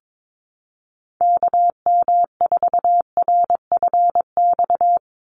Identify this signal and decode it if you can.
Morse code, heard as a keyed tone, about 22 wpm, 700 Hz. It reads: KM4RFX
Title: USA Call Signs 22wpm